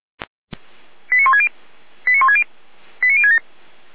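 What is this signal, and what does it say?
Five-tone selective call (ZVEI), heard as quick runs of five tones, tones of about 80 ms. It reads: 89160 89160 89067
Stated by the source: ZVEI five-tone selective call received by SDR, fire brigade feuerwehr-orlamuende.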